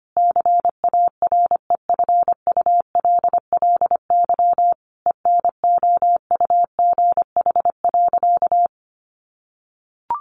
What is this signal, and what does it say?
Morse code, heard as a keyed tone, about 25 wpm, 700 Hz. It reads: CAREFULLY ENOUGH.
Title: Sentences from Top 1300 Words 25wpm